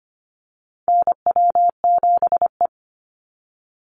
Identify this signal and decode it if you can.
Morse code, heard as a keyed tone, about 25 wpm, 700 Hz. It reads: NW7E
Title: NA SOTA Chaser Callsigns 25wpm